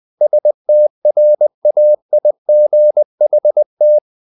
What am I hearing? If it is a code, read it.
Morse code, heard as a keyed tone, about 20 wpm, 600 Hz. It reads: STRAIGHT
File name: the old man and the sea / the old man and the sea part 1 20wpm